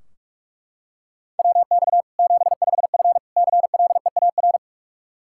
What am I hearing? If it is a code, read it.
Morse code, heard as a keyed tone, about 42 wpm, 700 Hz. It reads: WX 75F CLEAR
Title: Practice QSO - Exchange 2 - Info - 45wpm